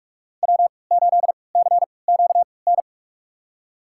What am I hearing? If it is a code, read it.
Morse code, heard as a keyed tone, about 43 wpm, spaced as for 25 wpm, 700 Hz. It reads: W8CQN
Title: USA Call Signs - 5 Characters - 3x Character Spacing - 45wpm